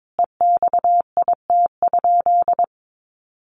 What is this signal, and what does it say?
Morse code, heard as a keyed tone, about 22 wpm, 700 Hz. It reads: EXIT?
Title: Sentences from Top 1100 Words 22wpm